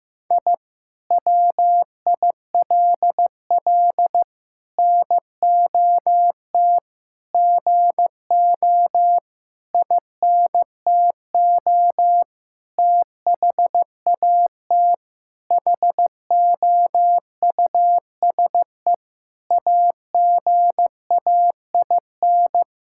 Morse code, heard as a keyed tone, about 15 wpm, 700 Hz. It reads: I WILL NOT GO INTO THAT HOUSE AGAIN
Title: Sentences from Top 1000 Words - Review Part 6 - 15wpm